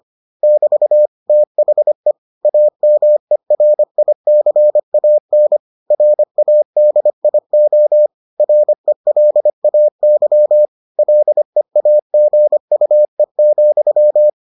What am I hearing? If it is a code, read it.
Morse code, heard as a keyed tone, about 25 wpm, 600 Hz. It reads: <BT> THE AMERICAN RADIO RELAY LEAGUE,